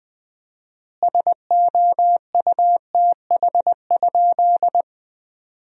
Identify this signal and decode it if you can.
Morse code, heard as a keyed tone, about 21 wpm, 700 Hz. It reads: SOUTH?
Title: 4x Word-Spacing - Sentences from Top 400 Words - 20wpm